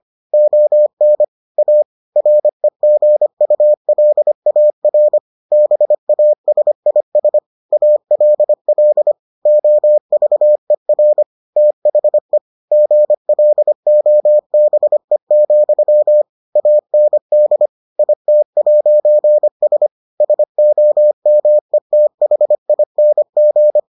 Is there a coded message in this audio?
Yes: Morse code from 0.3 to 23.8 s, heard as a keyed tone, about 25 wpm, 600 Hz. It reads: ON A REGULAR BASIS ALL OVER THE GLOBE, AND IT'S SOMETHING